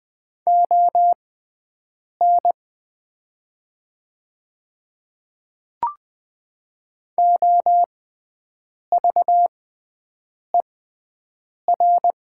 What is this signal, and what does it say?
Morse code, heard as a keyed tone, about 20 wpm, spaced as for 7 wpm, 700 Hz. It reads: ON OVER
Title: 6x Character Spacing - Top 100 Words - 20wpm